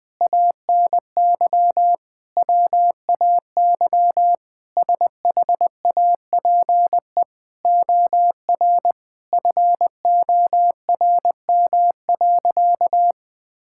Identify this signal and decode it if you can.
Morse code, heard as a keyed tone, about 20 wpm, 700 Hz. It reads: ANY WAY SHAPE OR FORM.
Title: Sentences with 6-Word Phrases - 20wpm